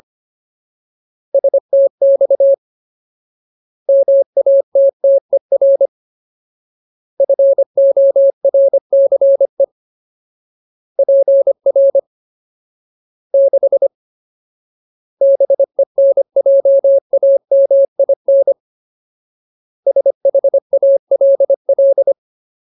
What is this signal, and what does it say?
Morse code, heard as a keyed tone, about 25 wpm, 550 Hz. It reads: STX MATTER FORCE PR 6 BENJAMIN SHALL
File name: Ultimate Word List / 25WPM Ultimate